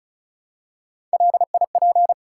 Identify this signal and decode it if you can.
Morse code, heard as a keyed tone, about 34 wpm, 700 Hz. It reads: LIP
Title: Three Letter Words - 35wpm